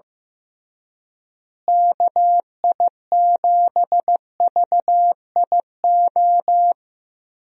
Morse code, heard as a keyed tone, about 15 wpm, 700 Hz. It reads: KI7VIO
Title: USA Call Signs - 6 Characters - Repeated 3x - 15wpm